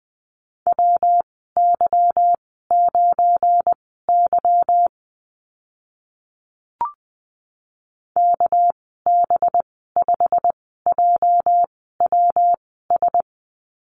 Morse code, heard as a keyed tone, about 20 wpm, spaced as for 14 wpm, 700 Hz. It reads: WY9Y KB5JWS
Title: 219 Lesson 35g - Number 0 - Call Signs 20wpm